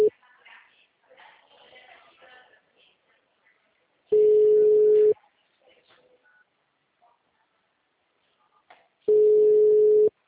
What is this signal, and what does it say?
Ringback tone (Europe, 425 Hz), with speech.